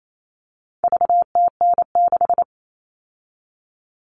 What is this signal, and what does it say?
Morse code, heard as a keyed tone, about 28 wpm, 700 Hz. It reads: VTN6
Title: All Cut Numbers - 4-Digit 28wpm